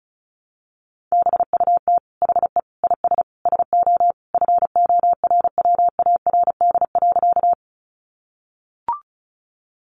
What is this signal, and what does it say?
Morse code, heard as a keyed tone, about 34 wpm, 700 Hz. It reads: BUT HE IS SO FORWARD.
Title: Sentences from Top 1300 Words 35wpm